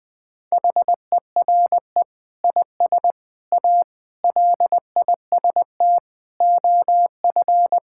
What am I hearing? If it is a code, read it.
Morse code, heard as a keyed tone, about 19 wpm, 700 Hz. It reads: HERE IS A LIST OF
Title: Sentences with 8-Word Phrases - 20wpm